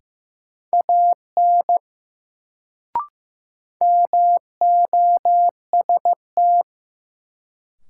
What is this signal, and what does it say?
Morse code, heard as a keyed tone, about 15 wpm, 700 Hz. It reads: AN MOST
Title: Sets of 2 Words - New Word Drill - 15wpm